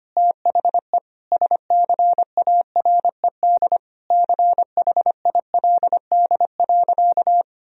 Morse code, heard as a keyed tone, about 25 wpm, 700 Hz. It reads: THE SCARED CHILD.
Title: Sentences from Top 1000 Words - Review Part 4 - 25wpm